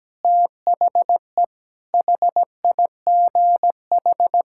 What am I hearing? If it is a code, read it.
Morse code, heard as a keyed tone, about 17 wpm, 700 Hz. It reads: THE HIGH
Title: Pangram Challenge v2 - 17wpm